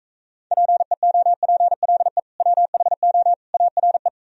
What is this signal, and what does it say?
Morse code, heard as a keyed tone, about 45 wpm, 700 Hz. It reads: PEOPLE WHO ARE